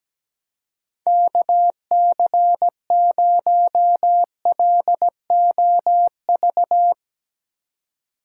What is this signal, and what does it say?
Morse code, heard as a keyed tone, about 17 wpm, 700 Hz. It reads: KC0LOV
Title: USA Call Signs - 6 Characters - 3x Character Spacing - 17wpm